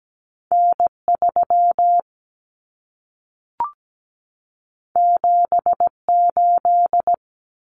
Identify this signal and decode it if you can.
Morse code, heard as a keyed tone, about 17 wpm, 700 Hz. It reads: N3 78
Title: Common Cut Numbers - 2-Digit 17wpm